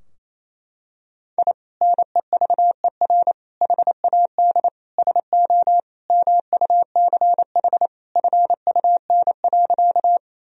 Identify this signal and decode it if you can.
Morse code, heard as a keyed tone, about 28 wpm, 700 Hz. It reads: I NEVER HAD SO MUCH FUN.
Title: Sentences from Top 1000 Words - Review Part 4 - 28wpm